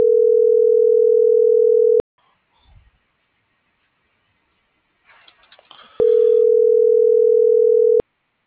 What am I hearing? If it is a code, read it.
Ringback tone (North America), 440 + 480 Hz.